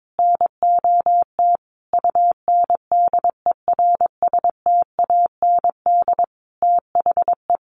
Morse code, heard as a keyed tone, about 22 wpm, 700 Hz. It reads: NOT UNDERSTAND THE
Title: Sentences from Top 600 Words 22wpm